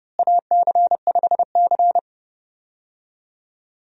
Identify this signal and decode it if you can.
Morse code, heard as a keyed tone, about 29 wpm, 700 Hz. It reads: AC5C
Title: USA Call Signs - 4 Characters - Repeated 2x - 30wpm